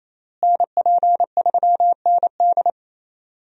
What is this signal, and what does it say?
Morse code, heard as a keyed tone, about 28 wpm, 700 Hz. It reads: NP3ND
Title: USA Call Signs - 5 Characters - Repeated 3x - 28wpm